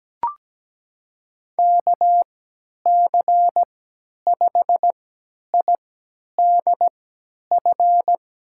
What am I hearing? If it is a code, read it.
Morse code, heard as a keyed tone, about 17 wpm, spaced as for 10 wpm, 700 Hz. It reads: KC5IDF
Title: USA Call Signs - 6 Characters - 3x Character Spacing - 17wpm